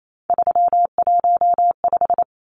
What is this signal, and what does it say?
Morse code, heard as a keyed tone, about 28 wpm, 700 Hz. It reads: 315
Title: Numbers - 3-Digit 28wpm